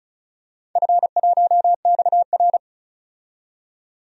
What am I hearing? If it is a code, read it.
Morse code, heard as a keyed tone, about 36 wpm, 690 Hz. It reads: F1XR
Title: DXCC-100 Call Signs - 4 Characters - 35wpm